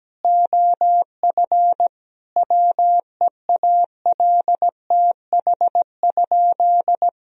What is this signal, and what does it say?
Morse code, heard as a keyed tone, about 17 wpm, 700 Hz. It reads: OF WEALTH?
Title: Sentences from Top 1000 Words - Review Part 4 - 17wpm